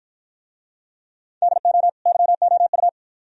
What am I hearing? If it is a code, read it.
Morse code, heard as a keyed tone, about 52 wpm, 700 Hz. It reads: DO YOU